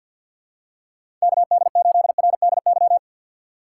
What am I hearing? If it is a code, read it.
Morse code, heard as a keyed tone, about 53 wpm, 700 Hz. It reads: KD8RDY
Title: USA Call Signs - 4x Character Spacing - 50wpm